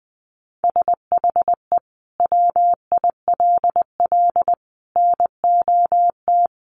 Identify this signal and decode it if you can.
Morse code, heard as a keyed tone, about 19 wpm, 700 Hz. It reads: SHE WILL NOT
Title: Sentences from Top 1500 Words 20wpm